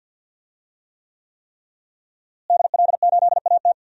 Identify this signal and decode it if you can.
Morse code, heard as a keyed tone, about 48 wpm, 700 Hz. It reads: DL8AT
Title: DXCC-100 Call Signs - 5 Characters - Repeated 3x - 50wpm